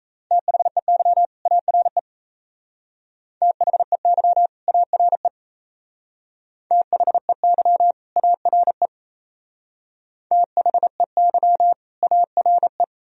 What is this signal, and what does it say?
Morse code, heard as a keyed tone, about 35 wpm, 690 Hz. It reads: THEY ARE THEY ARE THEY ARE THEY ARE
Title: Speed-Racing - Sets of 2 Words 28wpm